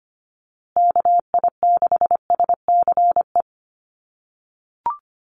Morse code, heard as a keyed tone, about 25 wpm, 700 Hz. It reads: KI6SCE
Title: USA Call Signs Repeated 2x 25wpm